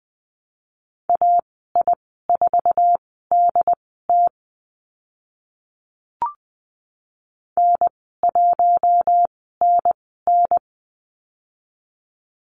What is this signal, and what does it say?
Morse code, heard as a keyed tone, about 19 wpm, spaced as for 14 wpm, 700 Hz. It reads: AI4DT N1NN
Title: 069 Lesson 12g - Letter d - Call Signs 20wpm